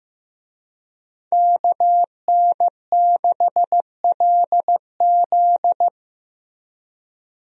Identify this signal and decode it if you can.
Morse code, heard as a keyed tone, about 16 wpm, 700 Hz. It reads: KN6LZ
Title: USA Call Signs - 5 Characters - 15wpm